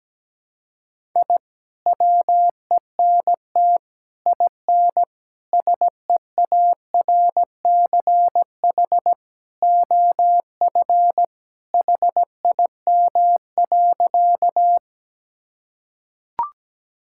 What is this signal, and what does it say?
Morse code, heard as a keyed tone, about 17 wpm, 700 Hz. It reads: I WENT IN SEARCH OF HIM.
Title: Sentences from Top 1000 Words - Review Part 5 - 17wpm